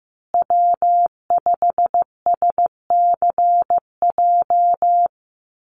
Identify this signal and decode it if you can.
Morse code, heard as a keyed tone, about 15 wpm, 700 Hz. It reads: W5SCJ
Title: USA Call Signs Repeated 2x 15wpm